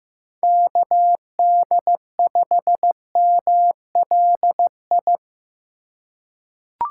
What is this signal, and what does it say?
Morse code, heard as a keyed tone, about 15 wpm, 700 Hz. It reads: KD5MLI
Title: USA Call Signs - 6 Characters - 8x Character Spacing - 15wpm